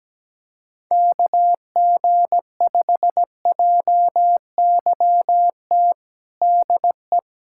Morse code, heard as a keyed tone, about 17 wpm, 700 Hz. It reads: KG5JYT DE
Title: USA Callsign DE Callsign 17wpm